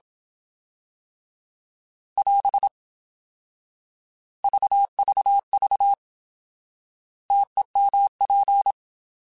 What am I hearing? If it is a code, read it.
Morse code, heard as a keyed tone, about 27 wpm, 800 Hz. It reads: <AS> VVV TEMP
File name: Alphabetical / Alphabetical mp3s / Ts @25